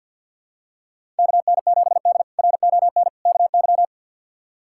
Morse code, heard as a keyed tone, about 52 wpm, 700 Hz. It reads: KN7D RON KY
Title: K1USN SST QSO Practice - 50wpm